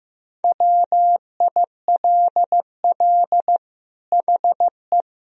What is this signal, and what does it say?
Morse code, heard as a keyed tone, about 16 wpm, 700 Hz. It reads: WILL HE
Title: Sentences from Top 1000 Words - Review Part 4 - 15wpm